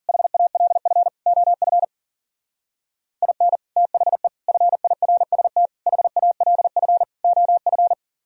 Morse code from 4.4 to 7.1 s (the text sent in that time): FIRST HALF